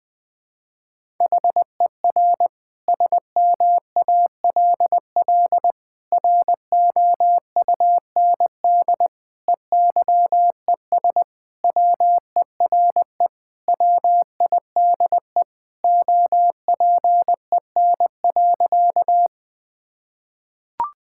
Morse code, heard as a keyed tone, about 20 wpm, 700 Hz. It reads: HER SMALL ROUND EYES WERE WIDE OPEN.
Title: Sentences from Top 1000 Words - Review Part 6 - 20wpm